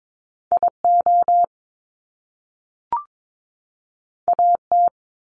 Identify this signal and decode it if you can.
Morse code, heard as a keyed tone, about 22 wpm, 700 Hz. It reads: IO AT
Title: Top Two Letter Combinations in Words 22wpm